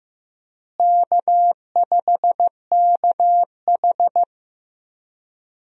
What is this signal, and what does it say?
Morse code, heard as a keyed tone, about 16 wpm, 700 Hz. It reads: K5KH
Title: USA Call Signs - 4 Characters - Repeated 2x - 15wpm